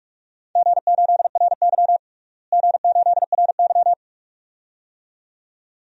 Morse code, heard as a keyed tone, about 42 wpm, 700 Hz. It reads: G8RY G8RY
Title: DXCC-100 Call Signs - 4 Characters - Repeated 2x - 45wpm